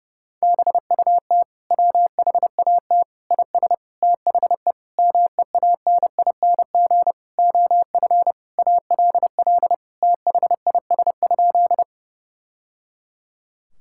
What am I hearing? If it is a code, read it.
Morse code, heard as a keyed tone, about 32 wpm, 700 Hz. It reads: BUT WHAT IS THE MEANING OF ALL THIS?
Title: Sentences from Top 1000 Words - Review Part 6 - 30wpm